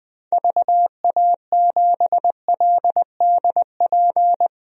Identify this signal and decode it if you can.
Morse code, heard as a keyed tone, about 21 wpm, 700 Hz. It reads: VA7LDP
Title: DXCC-100 Call Signs - 6 Characters - 20wpm